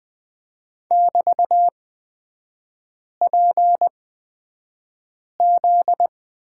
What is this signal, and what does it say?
Morse code, heard as a keyed tone, about 20 wpm, spaced as for 5 wpm, 700 Hz. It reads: <BT>PZ
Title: 243 Lesson 39e - Letter z - Three Characters 20wpm